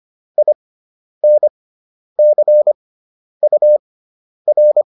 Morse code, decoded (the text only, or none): INCUR